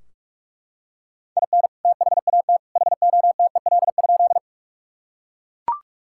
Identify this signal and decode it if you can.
Morse code, heard as a keyed tone, about 41 wpm, 700 Hz. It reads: IN THAT HOTEL?
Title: Sentences from Top 1600 Words 45wpm